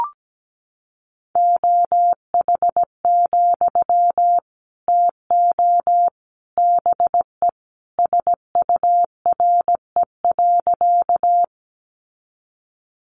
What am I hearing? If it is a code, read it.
Morse code, heard as a keyed tone, about 17 wpm, 700 Hz. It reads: OH, TO BE SURE.
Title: Sentences from Top 500 Words 17wpm